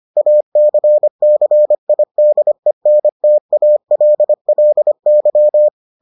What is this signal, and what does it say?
Morse code, heard as a keyed tone, about 25 wpm, 600 Hz. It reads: ACCIDENTALLY